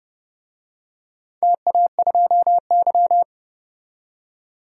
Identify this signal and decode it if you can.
Morse code, heard as a keyed tone, about 30 wpm, 700 Hz. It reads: TA2Y
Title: DXCC-100 Call Signs - 4 Characters - 30wpm